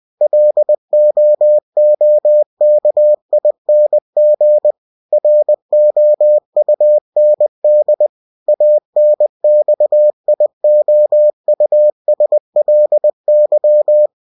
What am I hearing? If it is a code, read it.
Morse code, heard as a keyed tone, about 20 wpm, 600 Hz. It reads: LOOKING ROUND ANXIOUSLY
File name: cw-books / winnie-the-pooh / winnie-20wpm-page-0071